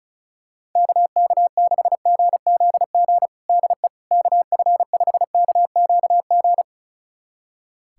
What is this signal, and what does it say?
Morse code, heard as a keyed tone, about 34 wpm, 700 Hz. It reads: KK6GZG DE KF5KQG
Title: USA Callsign DE Callsign - 6 Characters - 35wpm